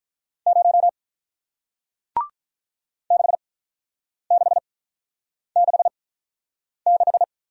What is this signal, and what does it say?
Morse code, heard as a keyed tone, about 48 wpm, spaced as for 9 wpm, 700 Hz. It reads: 0 6666